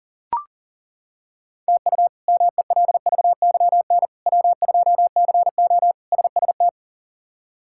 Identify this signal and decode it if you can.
Morse code, heard as a keyed tone, about 40 wpm, 700 Hz. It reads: TU MELVYN W2CO SST
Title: K1USN SST QSO Practice - 40wpm